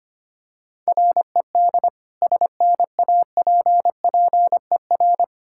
Morse code, heard as a keyed tone, about 25 wpm, 700 Hz. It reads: RED SNAPPER
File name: practice / 25wpm / Difficult Words with Hints - 25wpm